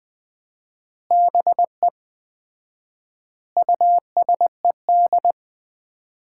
Morse code, heard as a keyed tone, about 20 wpm, 700 Hz. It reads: BE USED
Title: 4x Word-Spacing - Sets of 4 Words - 20wpm